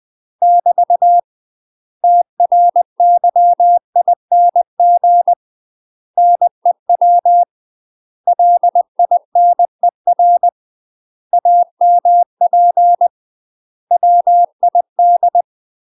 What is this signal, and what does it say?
Morse code, heard as a keyed tone, about 20 wpm, 700 Hz. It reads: <BT> TRYING NEW LINER AMP WID
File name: UK MASTER QSOs TRAINING / 20WPM / UK MASTER QSOs 0008